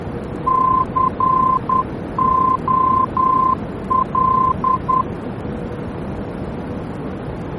Morse code, heard as a keyed tone, about 10 wpm, 1040 Hz, with noise. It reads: COL
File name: germany / mx / COL-D-VOR